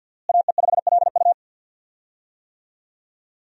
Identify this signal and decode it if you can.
Morse code, heard as a keyed tone, about 52 wpm, 700 Hz. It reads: AE5LU
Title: USA Call Signs - 5 Characters - Repeated 3x - 50wpm